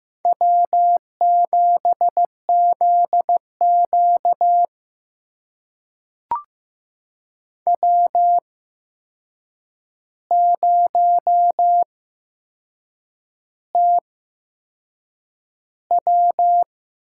Morse code, heard as a keyed tone, about 15 wpm, spaced as for 6 wpm, 700 Hz. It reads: W7ZQ W0TW
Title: USA Call Signs - 8x Character Spacing - 15wpm